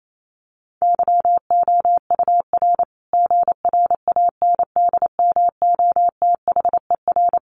Morse code, heard as a keyed tone, about 28 wpm, 700 Hz. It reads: YOUR GRANDMOTHER
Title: Sentences from Top 1200 Words 28wpm